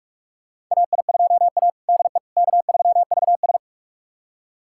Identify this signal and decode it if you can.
Morse code, heard as a keyed tone, about 46 wpm, 690 Hz. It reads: AI2A DE K3VS
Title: USA Callsign DE Callsign - 4 Characters - 45wpm